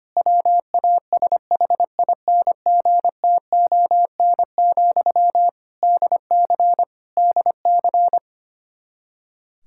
Morse code, heard as a keyed tone, about 25 wpm, 700 Hz. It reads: WASHINGTON, DC DC
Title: Practice QSO - Exchange 1 - Beginning - 25wpm